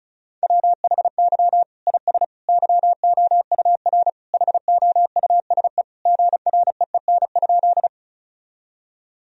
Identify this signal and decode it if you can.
Morse code, heard as a keyed tone, about 34 wpm, 700 Hz. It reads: WHY IS YOUR HOUSE GREEN?
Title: Sentences from Top 1000 Words - Review Part 2 - 35wpm